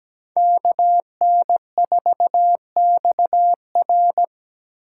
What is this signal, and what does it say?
Morse code, heard as a keyed tone, about 20 wpm, 700 Hz. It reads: KN4XR
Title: Speed Racing - USA 2x2 Call Signs 17wpm